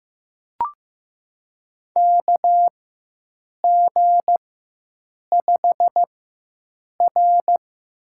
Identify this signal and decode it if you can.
Morse code, heard as a keyed tone, about 15 wpm, spaced as for 7 wpm, 700 Hz. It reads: KG5R
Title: USA Call Signs - 4x Character Spacing - 15wpm